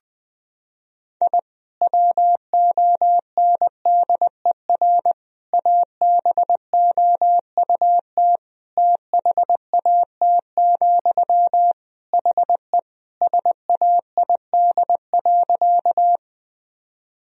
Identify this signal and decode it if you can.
Morse code, heard as a keyed tone, about 21 wpm, 700 Hz. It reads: I WONDER ABOUT THAT, HE SAID.
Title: Sentences from Top 1000 Words - Review Part 3 - 20wpm